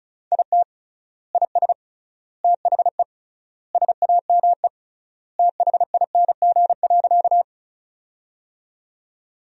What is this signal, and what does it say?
Morse code, heard as a keyed tone, about 36 wpm, 700 Hz. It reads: IT IS THE SAME THING.
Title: 3x Word-Spacing - Sentences from Top 200 Words - 35wpm